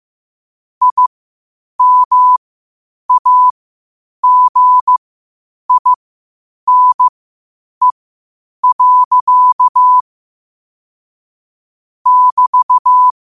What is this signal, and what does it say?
Morse code, heard as a keyed tone, about 15 wpm, spaced as for 8 wpm, 1000 Hz. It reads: IMAGINE. <BT>